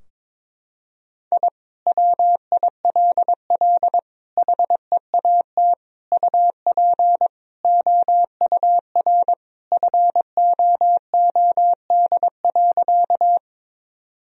Morse code, heard as a keyed tone, about 22 wpm, 700 Hz. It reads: I WILL HEAT UP OUR FOOD.
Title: Sentences from Top 1000 Words - Review Part 3 - 22wpm